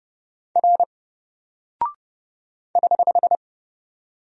Morse code, heard as a keyed tone, about 30 wpm, 700 Hz. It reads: R <HH>
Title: Single Character (Letter:Number:Punctuation:Prosign) 30wpm